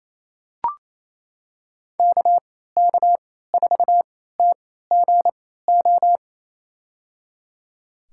Morse code, heard as a keyed tone, about 28 wpm, spaced as for 16 wpm, 700 Hz. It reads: KK4TGO
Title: USA Call Signs - 6 Characters - 3x Character Spacing - 28wpm